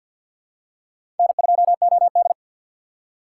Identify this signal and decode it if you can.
Morse code, heard as a keyed tone, about 52 wpm, 690 Hz. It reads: N2OD
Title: USA Call Signs - 3x Character Spacing - 50wpm